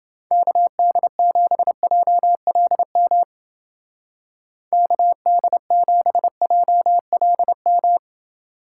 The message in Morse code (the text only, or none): KD7JLM KD7JLM